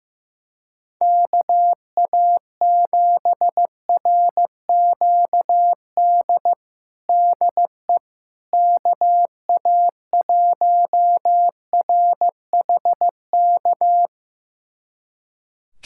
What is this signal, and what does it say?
Morse code, heard as a keyed tone, about 15 wpm, 700 Hz. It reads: KA7RQD DE KA1RHK
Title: USA Callsign DE Callsign - 6 Characters - 15wpm